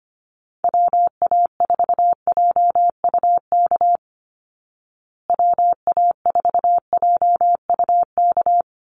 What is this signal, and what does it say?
Morse code, heard as a keyed tone, about 25 wpm, 700 Hz. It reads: WA4JUK WA4JUK
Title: Call Signs Repeated 3x - 25wpm